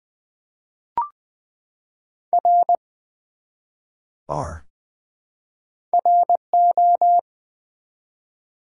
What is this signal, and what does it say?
Morse code, heard as a keyed tone, about 20 wpm, 700 Hz. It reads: R RO
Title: Build A Word - 1 Letter At A Time - 3 to 5-Letter Words - 20wpm